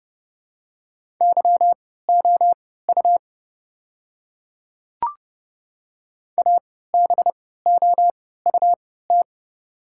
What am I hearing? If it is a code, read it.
Morse code, heard as a keyed tone, about 29 wpm, spaced as for 17 wpm, 700 Hz. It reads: YOU ABOUT
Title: 3x Character Spacing - Top 200 Words - 30wpm